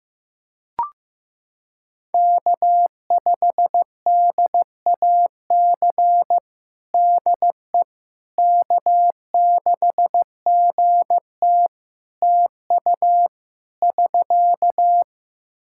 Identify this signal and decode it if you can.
Morse code, heard as a keyed tone, about 15 wpm, 700 Hz. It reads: K5DAC DE K6GT TU <SK>
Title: Practice QSO - Exchange 4 - Ending - 15wpm